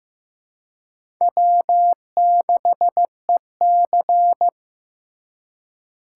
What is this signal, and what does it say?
Morse code, heard as a keyed tone, about 18 wpm, 700 Hz. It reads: W6EC
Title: Speed Racing - USA 1x2 Call Signs 15wpm